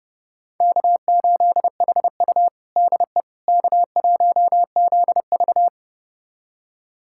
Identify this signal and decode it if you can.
Morse code, heard as a keyed tone, about 30 wpm, 700 Hz. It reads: K8HU DE K1ZV
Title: USA Callsign DE Callsign - 4 Characters - 30wpm